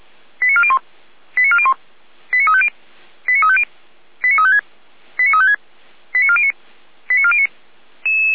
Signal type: five-tone selective call